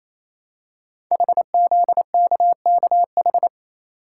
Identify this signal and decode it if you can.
Morse code, heard as a keyed tone, about 29 wpm, 700 Hz. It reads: HZKKH
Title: Code Groups - 5 Letters - 28wpm